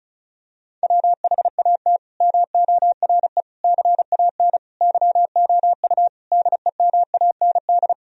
Morse code, heard as a keyed tone, about 36 wpm, 700 Hz. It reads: WHAT MORE CAN YOU DEMAND
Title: Sentences from Top 1000 Words - Review Part 4 - 35wpm